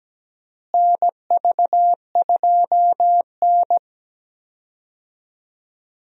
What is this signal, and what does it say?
Morse code, heard as a keyed tone, about 17 wpm, 700 Hz. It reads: NV2N
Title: USA Call Signs - 4 Characters - Repeated 3x - 17wpm